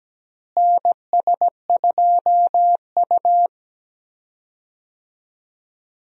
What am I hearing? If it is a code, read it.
Morse code, heard as a keyed tone, about 17 wpm, 700 Hz. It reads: NS2U
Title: USA Call Signs - 4 Characters - Repeated 2x - 17wpm